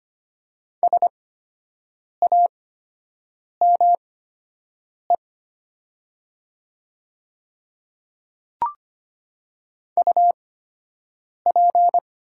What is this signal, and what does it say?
Morse code, heard as a keyed tone, about 25 wpm, spaced as for 7 wpm, 700 Hz. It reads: SAME UP
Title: 8x Character Spacing - Top 200 Words - 25wpm